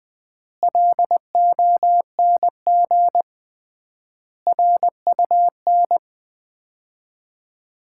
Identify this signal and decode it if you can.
Morse code, heard as a keyed tone, about 20 wpm, 700 Hz. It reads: LONG RUN
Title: 3x Word-Spacing - Sets of 4 Words - 20wpm